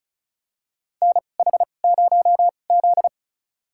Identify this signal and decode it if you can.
Morse code, heard as a keyed tone, about 36 wpm, spaced as for 25 wpm, 700 Hz. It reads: NH0Z
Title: USA Call Signs - 4 Characters - 2x Character Spacing - 35wpm